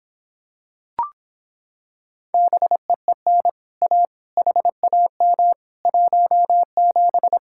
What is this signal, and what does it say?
Morse code, heard as a keyed tone, about 26 wpm, 700 Hz. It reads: BEEN A HAM 17